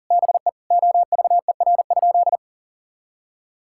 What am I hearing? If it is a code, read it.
Morse code, heard as a keyed tone, about 40 wpm, 700 Hz. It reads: BE OVER?
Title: Sentences from Top 1000 Words - Review Part 6 - 40wpm